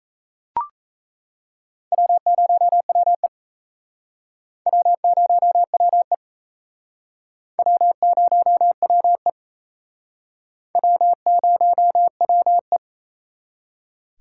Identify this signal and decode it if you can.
Morse code, heard as a keyed tone, about 36 wpm, 700 Hz. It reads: W0WE W0WE W0WE W0WE